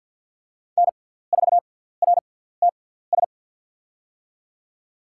Morse code, heard as a keyed tone, about 50 wpm, spaced as for 17 wpm, 700 Hz. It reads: N4RTS
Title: USA Call Signs - 5 Characters - 6x Character Spacing - 50wpm